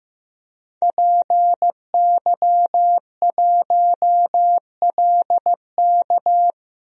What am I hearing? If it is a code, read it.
Morse code, heard as a keyed tone, about 15 wpm, 700 Hz. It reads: PY1LK
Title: DXCC-100 Call Signs - 5 Characters - 15wpm